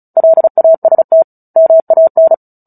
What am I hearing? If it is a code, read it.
Morse code, heard as a keyed tone, about 34 wpm, 650 Hz. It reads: LAST MAN
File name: Book Two / 09-Chapter 9 - 35 WPM